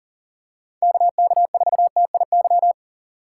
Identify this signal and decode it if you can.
Morse code, heard as a keyed tone, about 39 wpm, 700 Hz. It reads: KK4TIY
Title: USA Call Signs - 6 Characters - 3x Character Spacing - 40wpm